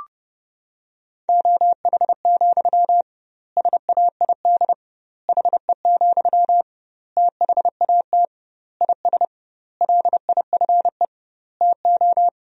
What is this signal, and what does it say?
Morse code, heard as a keyed tone, about 30 wpm, 700 Hz. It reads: OH, SAID HE, THAT IS LIFE TO
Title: 2x Word-Spacing - Sentences from Top 500 Words - 30wpm